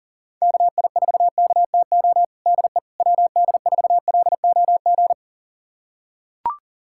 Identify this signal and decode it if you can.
Morse code, heard as a keyed tone, about 40 wpm, 700 Hz. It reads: KI4KTO DE WD4LOG
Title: USA Callsign DE Callsign - 6 Characters - 40wpm